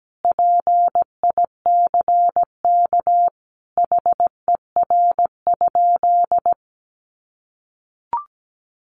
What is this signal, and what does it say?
Morse code, heard as a keyed tone, about 17 wpm, 700 Hz. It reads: PICK HER?
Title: Sentences from Top 1000 Words 17wpm